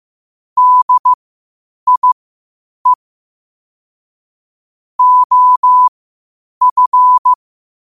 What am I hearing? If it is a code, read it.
Morse code, heard as a keyed tone, about 15 wpm, spaced as for 8 wpm, 1000 Hz. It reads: DIE OF
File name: qotdfiles / 8wpm